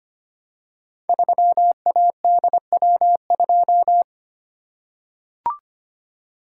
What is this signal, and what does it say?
Morse code, heard as a keyed tone, about 25 wpm, 700 Hz. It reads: 3ADW2
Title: Code Groups - 5 Letter:Numbers - 25wpm